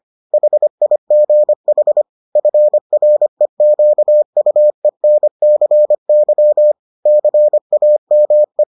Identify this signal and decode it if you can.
Morse code, heard as a keyed tone, about 25 wpm, 600 Hz. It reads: HIGH FREQUENCY CAME